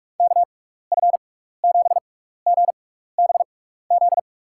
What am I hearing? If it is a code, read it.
Morse code, heard as a keyed tone, about 46 wpm, spaced as for 16 wpm, 700 Hz. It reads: KF7GBZ
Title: USA Call Signs - 6x Character Spacing - 45wpm